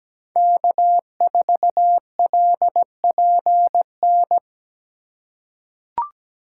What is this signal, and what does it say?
Morse code, heard as a keyed tone, about 17 wpm, 700 Hz. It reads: K4LPN
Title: USA Call Signs - 5 Characters - 3x Character Spacing - 17wpm